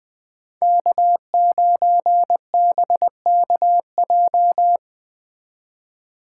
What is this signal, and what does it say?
Morse code, heard as a keyed tone, about 20 wpm, 700 Hz. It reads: K9BKJ
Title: Speed Racing - USA 1x3 Call Signs 17wpm